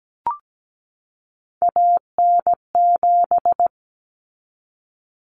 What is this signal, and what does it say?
Morse code, heard as a keyed tone, about 17 wpm, 700 Hz. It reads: AN7
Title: Common Cut Numbers - 3-Digit 17wpm